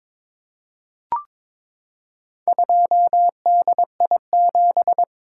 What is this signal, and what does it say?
Morse code, heard as a keyed tone, about 22 wpm, 700 Hz. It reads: 2DI7
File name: practice / 22wpm / Four Letter-Number Combinations - 22wpm